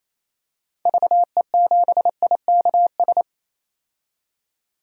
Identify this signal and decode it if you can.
Morse code, heard as a keyed tone, about 28 wpm, 700 Hz. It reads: VE7IKS
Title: DXCC-100 Call Signs - 6 Characters - 28wpm